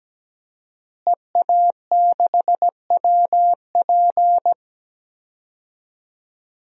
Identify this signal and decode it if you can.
Morse code, heard as a keyed tone, about 17 wpm, 700 Hz. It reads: EA6WP
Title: DXCC-100 Call Signs - 5 Characters - Repeated 3x - 17wpm